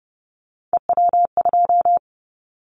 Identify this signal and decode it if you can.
Morse code, heard as a keyed tone, about 30 wpm, 700 Hz. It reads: EW2
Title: Callsign Prefixes for Top 100 DXCC - 30wpm